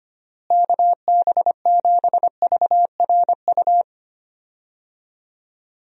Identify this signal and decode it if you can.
Morse code, heard as a keyed tone, about 25 wpm, 700 Hz. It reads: KB7VRU
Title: USA Call Signs - 6 Characters - Repeated 2x - 25wpm